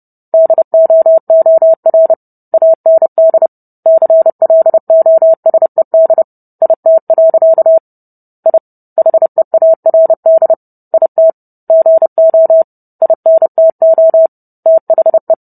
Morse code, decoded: DOOR AND CLOSED IT. I HEARD IT GO INTO THE

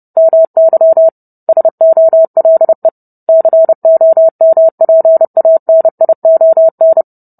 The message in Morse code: MY SOLE COMPANION

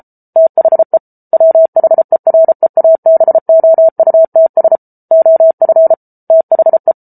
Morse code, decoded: THE WHEREABOUTS OF THE

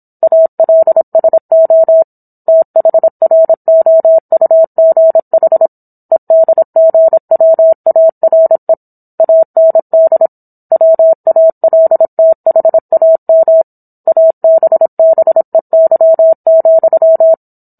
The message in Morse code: ALSO THROUGH EDGWARE AND WALTHAM ABBEY,

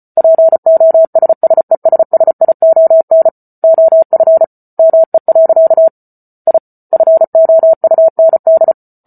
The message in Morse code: POSSESSION OF ME. I FOUND

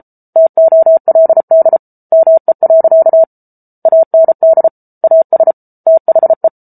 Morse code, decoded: TOLD ME. AND AS THE

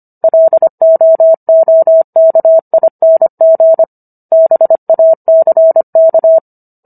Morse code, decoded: LOOKING BACK

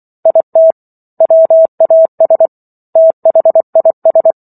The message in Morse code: IT WAS THIS

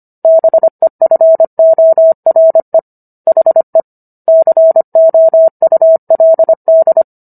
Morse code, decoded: BEFORE HE COULD

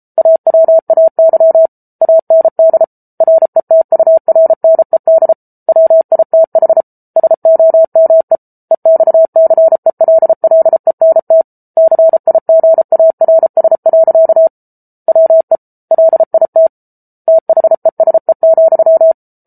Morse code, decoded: AWAY AND RETURNED WITH SOME EXCELLENT CIGARS. WE LIT THESE,

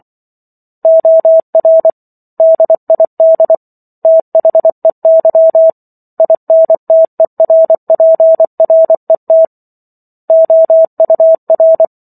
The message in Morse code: OR DID THEY INTERPRET OUR

